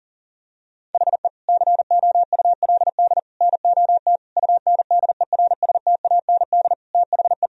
Morse code, heard as a keyed tone, about 40 wpm, 700 Hz. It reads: HE COULD NOT UNDERSTAND THE
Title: Sentences from Top 1000 Words - Review Part 2 - 40wpm